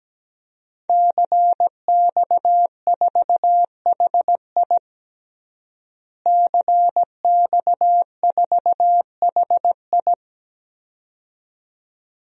Morse code, heard as a keyed tone, about 17 wpm, 700 Hz. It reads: CX4HI CX4HI
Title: DXCC-100 Call Signs - 5 Characters - Repeated 2x - 17wpm